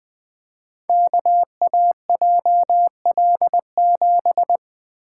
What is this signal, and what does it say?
Morse code, heard as a keyed tone, about 20 wpm, 700 Hz. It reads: KAJL7
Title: Code Groups - 5 Letter:Numbers - 20wpm